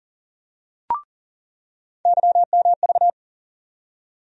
Morse code, heard as a keyed tone, about 40 wpm, 700 Hz. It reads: YMV
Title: USA Call Sign Postfixes - 3 Letters 40wpm